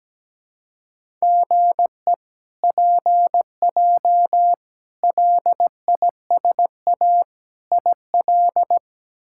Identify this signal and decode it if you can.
Morse code, heard as a keyed tone, about 17 wpm, 700 Hz. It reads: GE PJ LISA IL
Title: K1USN SST QSO Practice - 17wpm